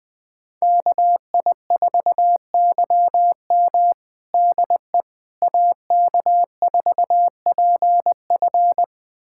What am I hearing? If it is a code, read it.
Morse code, heard as a keyed tone, about 20 wpm, 700 Hz. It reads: KI4YM DE AK4PF
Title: USA Callsign DE Callsign - 5 Characters - 20wpm